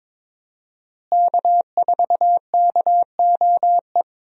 Morse code, heard as a keyed tone, about 22 wpm, 700 Hz. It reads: K4KOE
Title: USA Call Signs - 5 Characters - Repeated 3x - 22wpm